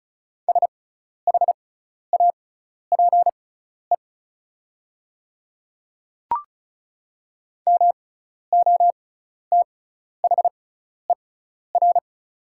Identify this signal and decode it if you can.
Morse code, heard as a keyed tone, about 36 wpm, spaced as for 12 wpm, 700 Hz. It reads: SHAPE MOTHER
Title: 6x Character Spacing - Top 500 Words - 35wpm